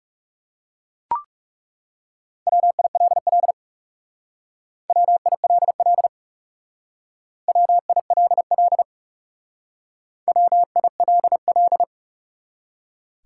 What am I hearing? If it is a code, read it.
Morse code, heard as a keyed tone, about 38 wpm, 700 Hz. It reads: WILL WILL WILL WILL